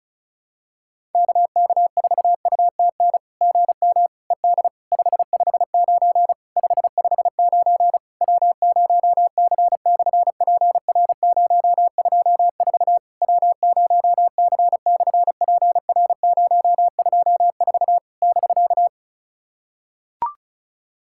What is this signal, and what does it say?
Morse code, heard as a keyed tone, about 36 wpm, 700 Hz. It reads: KK4UTN GM ED 559 559 W0C/PR024 W0C/PR024 <BK>
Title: SOTA QSO Practice 35wpm